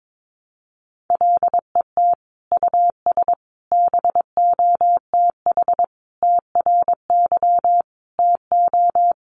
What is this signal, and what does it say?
Morse code, heard as a keyed tone, about 22 wpm, 700 Hz. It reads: LET US BOTH TRY TO
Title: Sentences from Top 1200 Words 22wpm